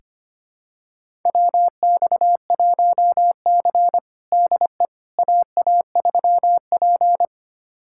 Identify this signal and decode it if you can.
Morse code, heard as a keyed tone, about 25 wpm, 700 Hz. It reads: WX1C DE AA3P
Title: USA Callsign DE Callsign - 4 Characters - 25wpm